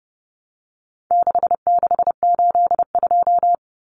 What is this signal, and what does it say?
Morse code, heard as a keyed tone, about 30 wpm, 700 Hz. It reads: 6682
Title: Numbers - 4-Digit 30wpm